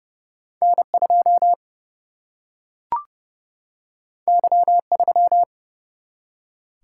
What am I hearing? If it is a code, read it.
Morse code, heard as a keyed tone, about 30 wpm, 700 Hz. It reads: N 2 Y 3
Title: Single-Letter Single-Number 30wpm